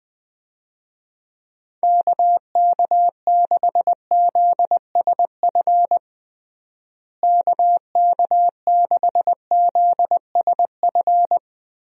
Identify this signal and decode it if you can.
Morse code, heard as a keyed tone, about 20 wpm, 700 Hz. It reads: KK6ZSF KK6ZSF
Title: USA Call Signs - 6 Characters - Repeated 2x - 20wpm